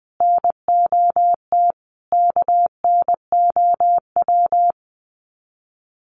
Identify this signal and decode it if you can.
Morse code, heard as a keyed tone, about 20 wpm, 700 Hz. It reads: NOT KNOW
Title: Sets of 5 Words Encore2 20wpm